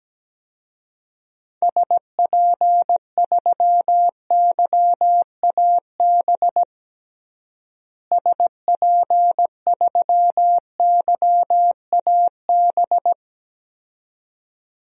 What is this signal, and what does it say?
Morse code, heard as a keyed tone, about 17 wpm, 700 Hz. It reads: SP3YAB SP3YAB
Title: DXCC-100 Call Signs - 6 Characters - Repeated 2x - 17wpm